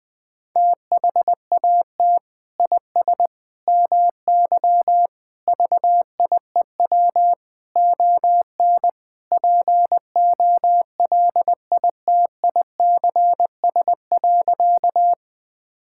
Morse code, heard as a keyed tone, about 20 wpm, 700 Hz. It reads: THAT IS MY VIEW ON POLITICS.
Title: Sentences from Top 1000 Words - Review Part 6 - 20wpm